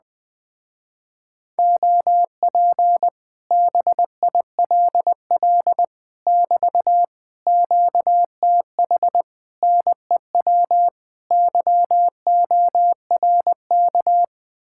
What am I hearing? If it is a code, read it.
Morse code, heard as a keyed tone, about 20 wpm, 700 Hz. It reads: OP BILL <BT> QTH NEW YORK
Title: Speed-Racing Common QSO Phrases - 17wpm